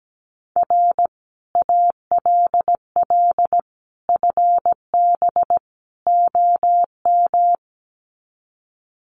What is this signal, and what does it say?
Morse code, heard as a keyed tone, about 17 wpm, 700 Hz. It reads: R ALL FB OM
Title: Common QSO Phrases 17wpm